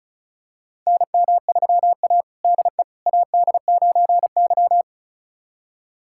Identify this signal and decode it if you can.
Morse code, heard as a keyed tone, about 36 wpm, 700 Hz. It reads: NM3A DE AD9Y
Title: USA Callsign DE Callsign - 4 Characters - 35wpm